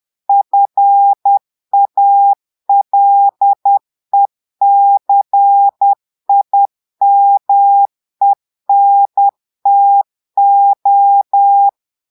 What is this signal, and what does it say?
Morse code, heard as a keyed tone, about 10 wpm, 800 Hz. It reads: FALECIMENTO
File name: Portugais - texte en clair 1 - vitesse - 10 mots minute